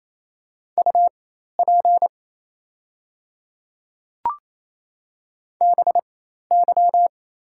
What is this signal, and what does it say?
Morse code, heard as a keyed tone, about 28 wpm, spaced as for 13 wpm, 700 Hz. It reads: UP BY
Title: 4x Character Spacing - Top 100 Words - 28wpm